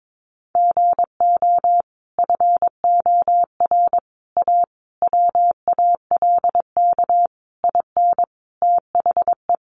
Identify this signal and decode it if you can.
Morse code, heard as a keyed tone, about 22 wpm, 700 Hz. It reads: GO FOR A WALK IN THE
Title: Sentences from Top 1300 Words 22wpm